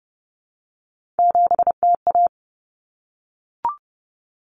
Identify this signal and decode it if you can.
Morse code, heard as a keyed tone, about 30 wpm, 700 Hz. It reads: 7TA
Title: Common Cut Numbers - 3-Digit 30wpm